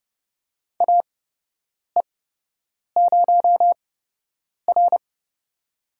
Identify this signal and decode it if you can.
Morse code, heard as a keyed tone, about 28 wpm, spaced as for 8 wpm, 700 Hz. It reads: AE0R
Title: USA Call Signs - 4 Characters - 8x Character Spacing - 30wpm